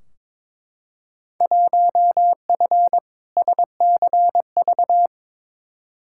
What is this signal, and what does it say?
Morse code, heard as a keyed tone, about 21 wpm, 700 Hz. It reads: 1F SCV
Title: Field Day Exchanges Encore - 22wpm